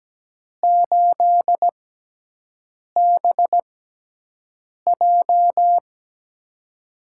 Morse code, heard as a keyed tone, about 17 wpm, spaced as for 6 wpm, 700 Hz. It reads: 8BJ
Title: USA Call Signs - 6x Character Spacing - 17wpm